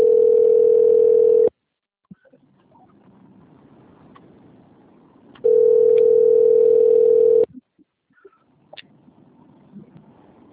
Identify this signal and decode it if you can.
Ringback tone (North America), 440 + 480 Hz.
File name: rg-1600-unknown- - - .90002